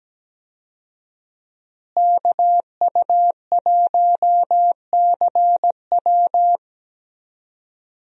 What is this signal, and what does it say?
Morse code, heard as a keyed tone, about 17 wpm, 700 Hz. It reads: KU1CW
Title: USA Call Signs - 5 Characters - Repeated 3x - 17wpm